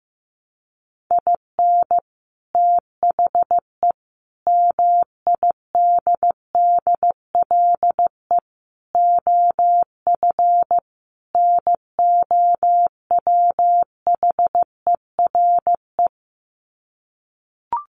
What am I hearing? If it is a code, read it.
Morse code, heard as a keyed tone, about 15 wpm, 700 Hz. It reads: IN THE MIDDLE OF NOWHERE
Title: Sets of 5 Words Encore 15wpm